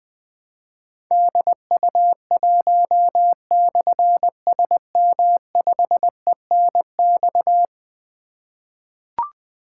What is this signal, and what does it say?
Morse code, heard as a keyed tone, about 20 wpm, 700 Hz. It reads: DU1/SM5ENX
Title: Difficult Call Signs 20wpm